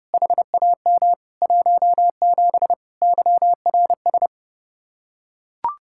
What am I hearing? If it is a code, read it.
Morse code, heard as a keyed tone, about 32 wpm, 700 Hz. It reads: HAM 17 YRS